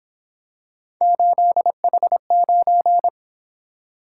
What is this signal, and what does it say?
Morse code, heard as a keyed tone, about 26 wpm, 700 Hz. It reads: 8H9
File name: Speed-Racing - Three Letter-Number Combinations - 22wpm